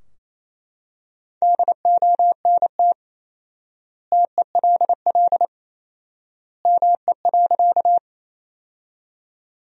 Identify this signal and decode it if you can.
Morse code, heard as a keyed tone, about 28 wpm, 700 Hz. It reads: DONT TELL ME.
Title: 4x Word-Spacing - Sentences from Top 300 Words - 28wpm